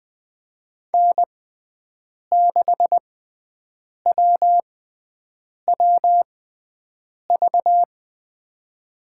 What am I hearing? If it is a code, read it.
Morse code, heard as a keyed tone, about 20 wpm, spaced as for 7 wpm, 700 Hz. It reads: N6WWV
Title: USA Call Signs - 5 Characters - 6x Character Spacing - 20wpm